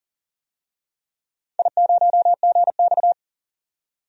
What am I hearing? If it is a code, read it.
Morse code, heard as a keyed tone, about 38 wpm, 700 Hz. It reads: I0GX
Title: DXCC-100 Call Signs - 4 Characters - 40wpm